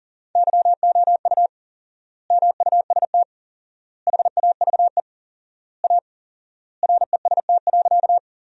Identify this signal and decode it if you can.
Morse code, heard as a keyed tone, about 37 wpm, 700 Hz. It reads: YOU MUST HAVE A REST.
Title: 4x Word-Spacing - Sentences from Top 500 Words - 40wpm